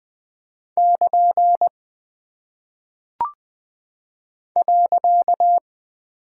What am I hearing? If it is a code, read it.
Morse code, heard as a keyed tone, about 20 wpm, 700 Hz. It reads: <KN> .
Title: Single Punctuation-Prosign - 20wpm